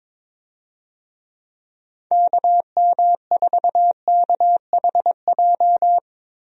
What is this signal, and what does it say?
Morse code, heard as a keyed tone, about 22 wpm, 700 Hz. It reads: KM4KHJ
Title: USA Call Signs - 6 Characters - 22wpm